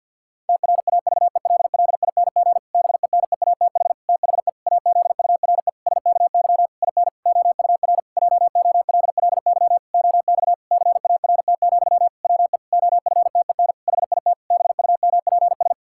Morse code, heard as a keyed tone, about 52 wpm, 700 Hz. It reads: TRAVELLING BENEATH THE AZURE SKY IN OUR JOLLY OX CART, WE OFTEN HIT BUMPS
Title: Pangram Challenge v2 - 50wpm